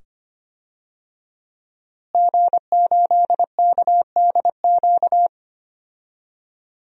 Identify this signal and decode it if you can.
Morse code, heard as a keyed tone, about 25 wpm, 700 Hz. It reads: G8KDQ
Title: DXCC-100 Call Signs - 5 Characters - Repeated 2x - 25wpm